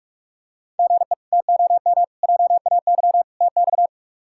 Morse code, heard as a keyed tone, about 43 wpm, 700 Hz. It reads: GE TOM JAY TX
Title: K1USN SST QSO Practice - 45wpm